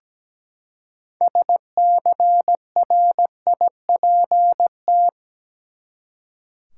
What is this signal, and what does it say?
Morse code, heard as a keyed tone, about 17 wpm, 700 Hz. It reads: SCRIPT
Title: Six Letter Words - 17wpm